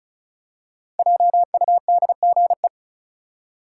Morse code, heard as a keyed tone, about 34 wpm, 700 Hz. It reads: JUDGE
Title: Five Letter Words - 35wpm